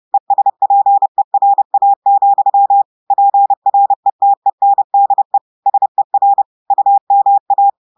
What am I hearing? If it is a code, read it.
Morse code, heard as a keyed tone, about 30 wpm, 800 Hz. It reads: ESPERA, PRETENDE SER UMA
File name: audios / portugais / portugais texte en clair 1 vitesse 30 mots minutes